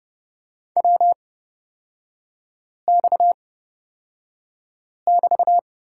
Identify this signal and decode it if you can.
Morse code, heard as a keyed tone, about 30 wpm, spaced as for 5 wpm, 690 Hz. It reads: WX<BT>
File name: Lesson 37e - Letter x - Three Characters 30wpm